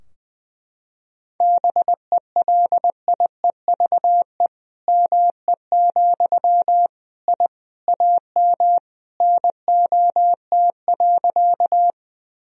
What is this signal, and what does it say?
Morse code, heard as a keyed tone, about 20 wpm, 700 Hz. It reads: BELIEVE ME, I AM NOT.
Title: Sentences from Top 1000 Words - Review Part 2 - 20wpm